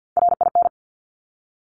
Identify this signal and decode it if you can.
Morse code, heard as a keyed tone, about 50 wpm, 700 Hz. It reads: RIN
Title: Top Three Letter Combinations in Words 50wpm